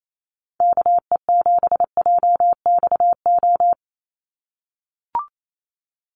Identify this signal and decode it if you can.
Morse code, heard as a keyed tone, about 28 wpm, 700 Hz. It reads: KE7JXO
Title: USA Call Signs 28wpm